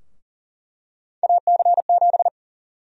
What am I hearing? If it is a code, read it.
Morse code, heard as a keyed tone, about 40 wpm, 700 Hz. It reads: AC7
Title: USA Call Sign Prefixes - 40wpm